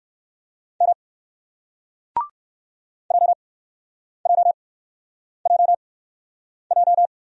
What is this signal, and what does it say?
Morse code, heard as a keyed tone, about 58 wpm, spaced as for 9 wpm, 700 Hz. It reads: M JJJJ